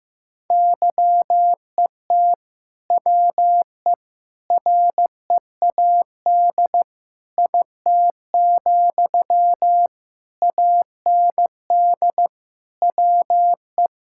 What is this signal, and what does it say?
Morse code, heard as a keyed tone, about 16 wpm, 700 Hz. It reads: YET WE READ IT, AND WE
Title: Sentences from Top 1600 Words 15wpm